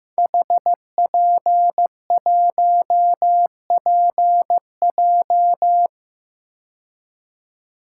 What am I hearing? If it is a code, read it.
Morse code, heard as a keyed tone, about 15 wpm, 700 Hz. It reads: HP1PJ
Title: Code Groups - 5 Letter:Numbers - 15wpm